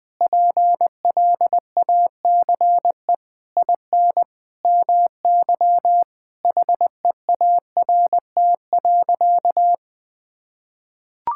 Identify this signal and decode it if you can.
Morse code, heard as a keyed tone, about 20 wpm, 700 Hz. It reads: PLACE IN MY HEART.
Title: Sentences with 9-Word Phrases - 20wpm